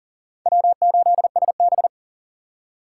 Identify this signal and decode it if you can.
Morse code, heard as a keyed tone, about 52 wpm, 700 Hz. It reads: W8SB